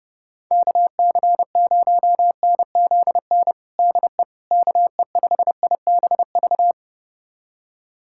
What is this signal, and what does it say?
Morse code, heard as a keyed tone, about 32 wpm, 700 Hz. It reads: KC0NZN DE KE5IBV
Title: USA Callsign DE Callsign 30wpm